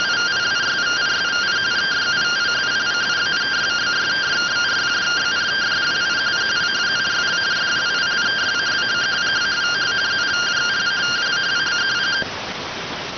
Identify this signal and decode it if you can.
Two-tone teleprinter signal, about 170 Hz shift.